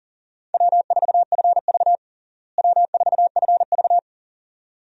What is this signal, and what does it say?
Morse code, heard as a keyed tone, about 37 wpm, 700 Hz. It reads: W4FV W4FV
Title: USA Call Signs - 4 Characters - Repeated 2x - 40wpm